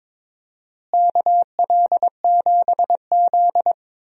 Morse code, heard as a keyed tone, about 22 wpm, 700 Hz. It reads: KL7Z
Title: DXCC-100 Call Signs - 4 Characters - Repeated 2x - 22wpm